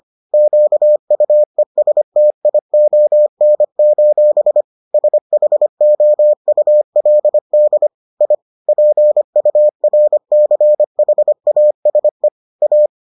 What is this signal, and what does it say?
Morse code, heard as a keyed tone, about 25 wpm, 600 Hz. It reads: QUESTION: SHOULD I PURCHASE A